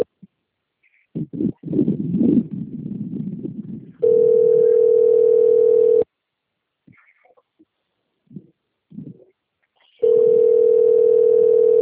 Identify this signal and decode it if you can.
Ringback tone (North America), 440 + 480 Hz.